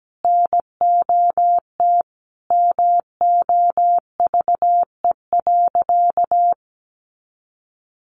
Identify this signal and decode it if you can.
Morse code, heard as a keyed tone, about 17 wpm, 700 Hz. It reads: NOT MOVE.
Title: A Sound Of Thunder 17wpm